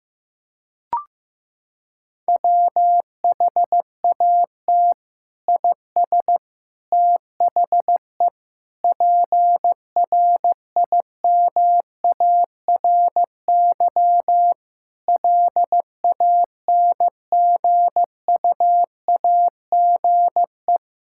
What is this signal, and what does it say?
Morse code, heard as a keyed tone, about 16 wpm, 700 Hz. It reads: WHAT IS THE PRIMARY LANGUAGE
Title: Sentences from Top 1000 Words - Review Part 5 - 15wpm